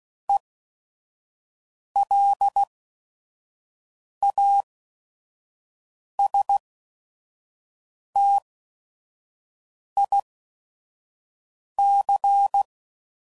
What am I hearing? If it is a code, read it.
Morse code, heard as a keyed tone, about 16 wpm, spaced as for 5 wpm, 800 Hz. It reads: ELASTIC
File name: Prefixes / By First Letter / Es